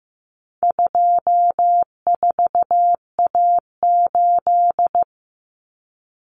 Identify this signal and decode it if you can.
Morse code, heard as a keyed tone, about 15 wpm, 700 Hz. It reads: 24A8
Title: Common Cut Numbers - 4-Digit 15wpm